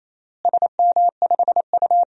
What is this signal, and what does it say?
Morse code, heard as a keyed tone, about 28 wpm, 700 Hz. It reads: SM5U
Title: DXCC-100 Call Signs - 4 Characters - Repeated 3x - 28wpm